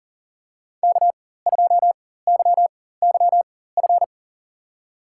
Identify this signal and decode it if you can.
Morse code, heard as a keyed tone, about 40 wpm, spaced as for 19 wpm, 700 Hz. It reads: K2YYF
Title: USA Call Signs - 4x Character Spacing - 40wpm